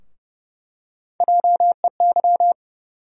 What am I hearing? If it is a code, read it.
Morse code, heard as a keyed tone, about 32 wpm, 700 Hz. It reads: JEY
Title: USA Call Sign Postfixes - 3 Letters 30wpm